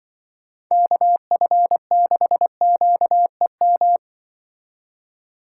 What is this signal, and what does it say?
Morse code, heard as a keyed tone, about 23 wpm, 700 Hz. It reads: KF6QEM
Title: Speed Racing - USA 2x3 Call Signs 20wpm